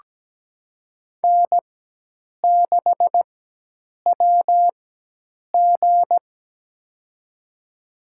Morse code, heard as a keyed tone, about 17 wpm, spaced as for 8 wpm, 700 Hz. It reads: N6WG
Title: USA Call Signs - 4 Characters - 4x Character Spacing - 17wpm